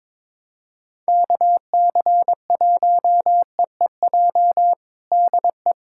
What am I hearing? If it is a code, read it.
Morse code, heard as a keyed tone, about 22 wpm, 700 Hz. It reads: KC1EEJ DE
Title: USA Callsign DE Callsign - 6 Characters - 22wpm